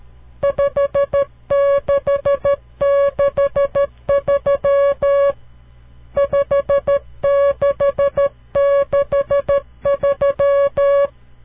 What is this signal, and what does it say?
Morse code, heard as a keyed tone, about 13 wpm, 560 Hz. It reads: 5663 5663
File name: CW Learning Programs / farnsworth / unit 5